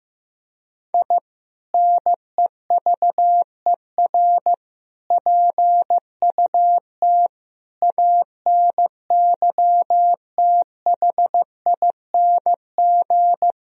Morse code, heard as a keyed tone, about 15 wpm, 700 Hz. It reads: I NEVER PUT ANYTHING
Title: Sentences from Top 1000 Words - Review Part 1 - 15wpm